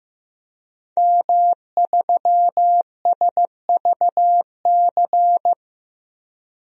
Morse code, heard as a keyed tone, about 15 wpm, 700 Hz. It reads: M3SVC
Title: DXCC-100 Call Signs - 5 Characters - 15wpm